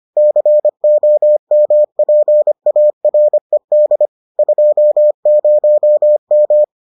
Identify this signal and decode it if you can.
Morse code, heard as a keyed tone, about 25 wpm, 600 Hz. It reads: COMPARED 20M